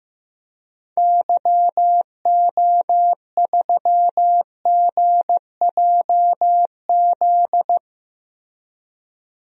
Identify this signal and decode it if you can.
Morse code, heard as a keyed tone, about 15 wpm, 700 Hz. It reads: YO3GJZ
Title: DXCC-100 Call Signs - 6 Characters - 15wpm